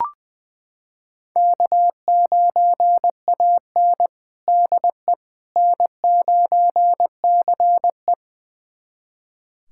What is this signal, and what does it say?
Morse code, heard as a keyed tone, about 21 wpm, 700 Hz. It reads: K9AN DE N9CE
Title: USA Callsign DE Callsign - 4 Characters - 20wpm